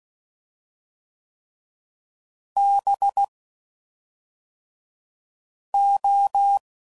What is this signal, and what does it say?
Morse code, heard as a keyed tone, about 16 wpm, 800 Hz. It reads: B O
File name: Suffixes / Special INGs / Fast INGs